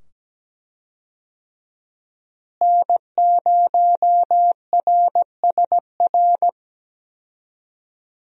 Morse code, heard as a keyed tone, about 17 wpm, 700 Hz. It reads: N0RSR
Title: USA Call Signs - 5 Characters - Repeated 2x - 17wpm